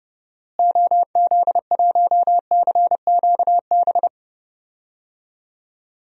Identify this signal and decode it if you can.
Morse code, heard as a keyed tone, about 30 wpm, 700 Hz. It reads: OZ1CQB
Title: DXCC-100 Call Signs - 6 Characters - Repeated 3x - 30wpm